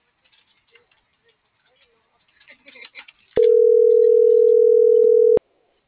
Ringback tone (North America), 440 + 480 Hz.
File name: CENTRAL TERRESTRE / out erick.sanjuan